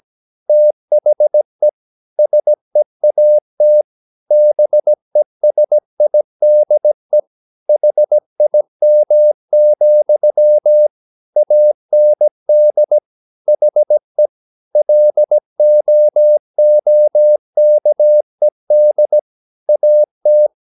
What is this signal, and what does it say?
Morse code, heard as a keyed tone, about 17 wpm, 600 Hz. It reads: THE SEAT BESIDE HIM, AND HE LOOKED AT
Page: Romanen Never af Ken Follet på engelsk.